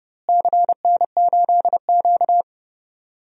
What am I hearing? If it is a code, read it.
Morse code, heard as a keyed tone, about 30 wpm, 700 Hz. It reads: CN8Q
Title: DXCC-100 Call Signs - 4 Characters - Repeated 2x - 30wpm